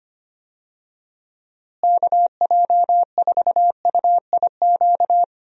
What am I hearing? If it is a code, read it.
Morse code, heard as a keyed tone, about 25 wpm, 700 Hz. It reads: KJ4UIQ
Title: USA Call Signs - 6 Characters - Repeated 2x - 25wpm